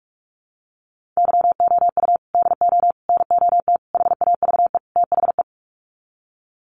Morse code, heard as a keyed tone, about 44 wpm, 700 Hz. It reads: YOU DO NOT HAVE THE
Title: Sets of 5 Words 45wpm